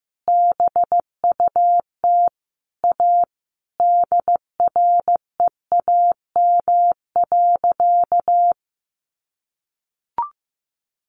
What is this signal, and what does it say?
Morse code, heard as a keyed tone, about 15 wpm, 700 Hz. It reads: BUT A DREAM.
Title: Sentences from Top 1500 Words 15wpm